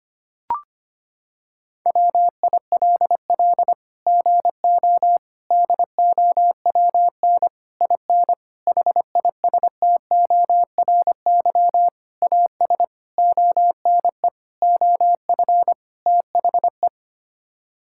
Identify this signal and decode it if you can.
Morse code, heard as a keyed tone, about 25 wpm, 700 Hz. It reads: WILL GO DOWN IN HISTORY AS ONE OF THE
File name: Sets of 9 Words - 25wpm